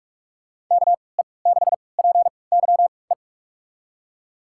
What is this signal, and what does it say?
Morse code, heard as a keyed tone, about 45 wpm, spaced as for 25 wpm, 700 Hz. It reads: KE6PYE
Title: USA Call Signs - 6 Characters - 3x Character Spacing - 45wpm